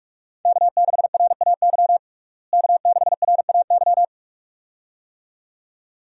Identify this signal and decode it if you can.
Morse code, heard as a keyed tone, about 43 wpm, 700 Hz. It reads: K6RAY K6RAY
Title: USA Call Signs - 5 Characters - Repeated 2x - 45wpm